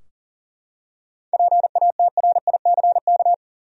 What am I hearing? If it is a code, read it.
Morse code, heard as a keyed tone, about 40 wpm, 700 Hz. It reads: PATRICK
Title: USA Common Names - 40wpm